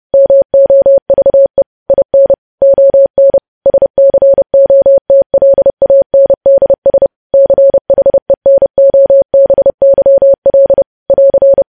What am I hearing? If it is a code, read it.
Morse code, heard as a keyed tone, about 30 wpm, 550 Hz. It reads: MOVE IN ON SCOTLANDS CHENOBYL <AR>
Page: news headlines in morse code at 30 wpm